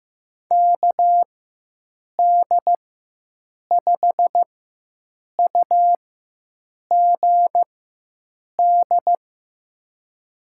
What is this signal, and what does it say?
Morse code, heard as a keyed tone, about 15 wpm, spaced as for 7 wpm, 700 Hz. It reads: KD5UGD
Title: USA Call Signs - 4x Character Spacing - 15wpm